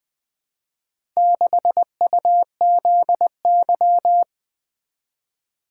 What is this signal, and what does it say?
Morse code, heard as a keyed tone, about 20 wpm, 700 Hz. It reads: 6UZY
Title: Four Letter-Number Combinations - 20wpm